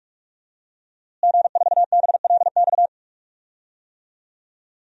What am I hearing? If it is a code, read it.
Morse code, heard as a keyed tone, about 46 wpm, 700 Hz. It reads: G4BLX
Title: DXCC-100 Call Signs - 5 Characters - Repeated 2x - 45wpm